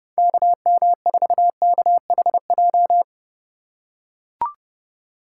Morse code, heard as a keyed tone, about 31 wpm, 700 Hz. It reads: KM4KHJ
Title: USA Call Signs - 4x Character Spacing - 30wpm